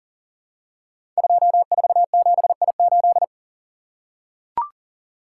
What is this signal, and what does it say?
Morse code, heard as a keyed tone, about 42 wpm, 690 Hz. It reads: 247I8
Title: Code Groups - 5 Letter:Numbers - 40wpm